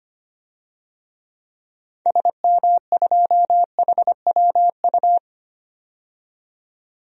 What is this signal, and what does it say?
Morse code, heard as a keyed tone, about 25 wpm, 700 Hz. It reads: SM2HWU
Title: DXCC-100 Call Signs - 6 Characters - 25wpm